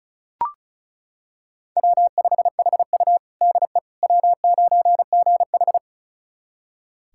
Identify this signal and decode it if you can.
Morse code, heard as a keyed tone, about 36 wpm, 700 Hz. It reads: W5HU DE W9GH
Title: USA Callsign DE Callsign - 4 Characters - 35wpm